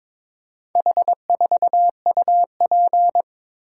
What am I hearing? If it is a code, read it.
Morse code, heard as a keyed tone, about 21 wpm, 700 Hz. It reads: H4UP
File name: practice / 22wpm / Four Letter-Number Combinations - 22wpm